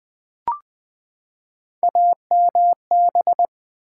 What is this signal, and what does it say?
Morse code, heard as a keyed tone, about 20 wpm, 700 Hz. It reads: AMB
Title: Build A Word - 3 Letters At A Time - 9-Letter Words - 20wpm